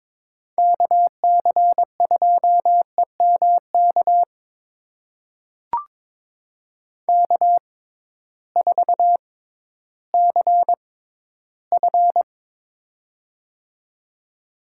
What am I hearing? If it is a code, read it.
Morse code, heard as a keyed tone, about 22 wpm, spaced as for 13 wpm, 700 Hz. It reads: KC2EMK K4CF
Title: USA Call Signs - 6x Character Spacing - 22wpm